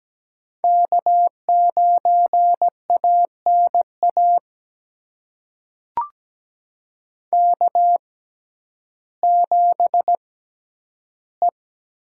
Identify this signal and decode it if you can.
Morse code, heard as a keyed tone, about 17 wpm, spaced as for 10 wpm, 700 Hz. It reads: K9ANA K7E
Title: USA Call Signs - 5 Characters - 6x Character Spacing - 17wpm